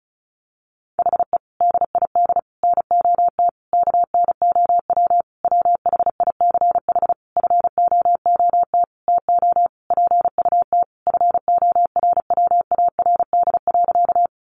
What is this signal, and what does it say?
Morse code, heard as a keyed tone, about 34 wpm, 700 Hz. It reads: HE DID NOT KNOW WHICH FOOT TO PUT FORWARD.
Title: Sentences from Top 1300 Words 35wpm